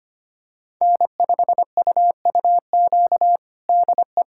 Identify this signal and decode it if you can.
Morse code, heard as a keyed tone, about 25 wpm, 700 Hz. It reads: N5UUQ DE
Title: USA Callsign DE Callsign 25wpm